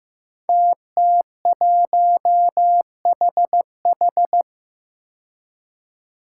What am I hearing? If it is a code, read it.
Morse code, heard as a keyed tone, about 15 wpm, 700 Hz. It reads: TT1HH
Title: Five Challenging Letter-Number Combinations 15wpm